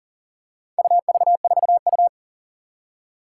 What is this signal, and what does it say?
Morse code, heard as a keyed tone, about 42 wpm, 700 Hz. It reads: UV4U
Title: Four Challenging Letter-Number Combinations 40wpm